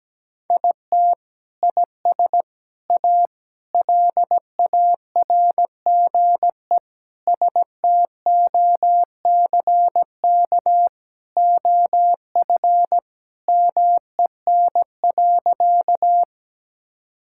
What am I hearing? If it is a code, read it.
Morse code, heard as a keyed tone, about 17 wpm, 700 Hz. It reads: IT IS A LARGE STOCK OF MEN.
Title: Sentences from Top 1000 Words - Review Part 5 - 17wpm